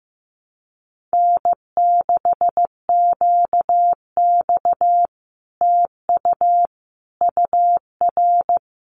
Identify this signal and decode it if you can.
Morse code, heard as a keyed tone, about 15 wpm, 700 Hz. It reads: N6QX TU UR
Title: Common QSO Phrases Encore 15wpm